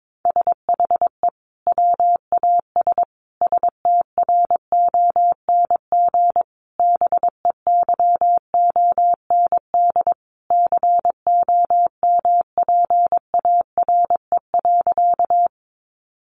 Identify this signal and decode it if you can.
Morse code, heard as a keyed tone, about 22 wpm, 700 Hz. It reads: SHE WAS STRONG BEYOND COMPARE.
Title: Sentences from Top 1300 Words 22wpm